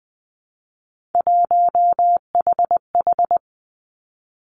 Morse code, heard as a keyed tone, about 20 wpm, 700 Hz. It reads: 1HH
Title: Three Challenging Letter-Number Combinations 20wpm